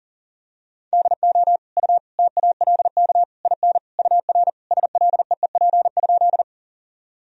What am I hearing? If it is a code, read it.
Morse code, heard as a keyed tone, about 40 wpm, 700 Hz. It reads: DO U TALK IN UR SLEEP?
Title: Sentences from Top 500 Words and Abbreviations 40wpm